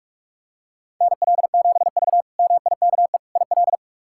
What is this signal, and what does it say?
Morse code, heard as a keyed tone, about 46 wpm, 700 Hz. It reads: NL7V MIKE IL
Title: K1USN SST QSO Practice - 45wpm